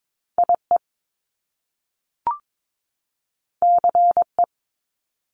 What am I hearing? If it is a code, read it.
Morse code, heard as a keyed tone, about 22 wpm, 700 Hz. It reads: IE CE
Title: Top Two Letter Combinations in Words 22wpm